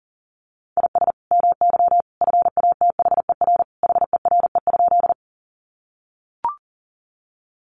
Morse code, heard as a keyed tone, about 40 wpm, 700 Hz. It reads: IS MY FATHER HERE?
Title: Sentences from Top 300 Words 40wpm